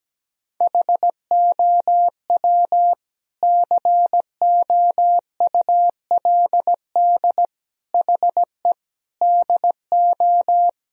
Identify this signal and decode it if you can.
Morse code, heard as a keyed tone, about 17 wpm, 700 Hz. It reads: HOW COULD HE DO
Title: Sentences from Top 1000 Words - Review Part 1 - 17wpm